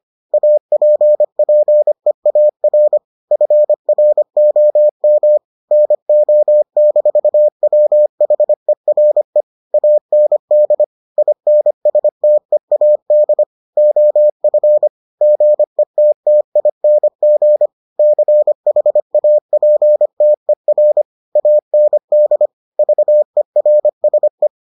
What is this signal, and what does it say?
Morse code, heard as a keyed tone, about 25 wpm, 600 Hz. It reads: APPEAR FROM NO-WHERE AND INSTEAD OF GETTING CHAPTER AND VERSE